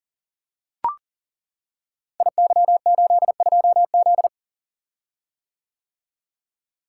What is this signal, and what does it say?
Morse code, heard as a keyed tone, about 40 wpm, 700 Hz. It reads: IY82Z
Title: Code Groups - 5 Letter:Numbers - 40wpm